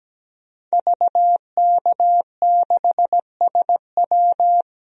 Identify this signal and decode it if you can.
Morse code, heard as a keyed tone, about 17 wpm, 700 Hz. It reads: VK6SW
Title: DXCC-100 Call Signs - 5 Characters - Repeated 2x - 17wpm